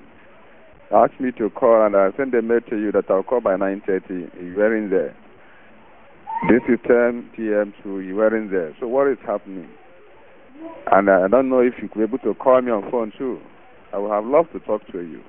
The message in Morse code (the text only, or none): none